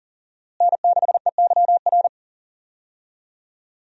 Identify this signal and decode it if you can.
Morse code, heard as a keyed tone, about 37 wpm, 700 Hz. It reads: N6EYR
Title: USA Call Signs - 5 Characters - Repeated 2x - 40wpm